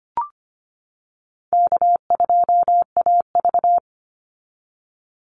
Morse code, heard as a keyed tone, about 25 wpm, 700 Hz. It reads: K2AV
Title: Call Signs - 25wpm